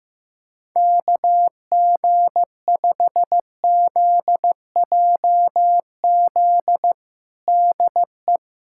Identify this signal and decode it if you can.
Morse code, heard as a keyed tone, about 15 wpm, 700 Hz. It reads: KG5ZJZ DE
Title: USA Callsign DE Callsign - 6 Characters - 15wpm